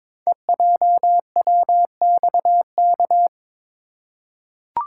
Morse code, heard as a keyed tone, about 22 wpm, 700 Hz. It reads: EJWXK
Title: Code Groups - 5 Letters - 22wpm